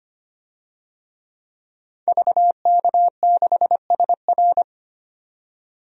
Morse code, heard as a keyed tone, about 25 wpm, 700 Hz. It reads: VK6SR
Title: DXCC-100 Call Signs - 5 Characters - 25wpm